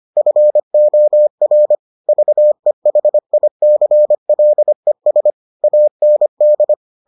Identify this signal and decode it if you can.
Morse code, heard as a keyed tone, about 25 wpm, 600 Hz. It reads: FOR VEHICLES AND